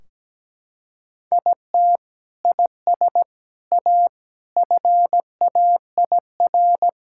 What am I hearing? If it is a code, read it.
Morse code, heard as a keyed tone, about 17 wpm, 700 Hz. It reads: IT IS A FAIR
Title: Sentences from Top 1600 Words 17wpm